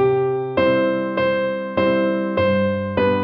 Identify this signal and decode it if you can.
There is no code.